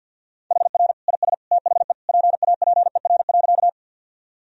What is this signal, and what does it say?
Morse code, heard as a keyed tone, about 50 wpm, 700 Hz. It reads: HR IS THE PAPER.
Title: Sentences from Top 300 Words and Abbreviations 50wpm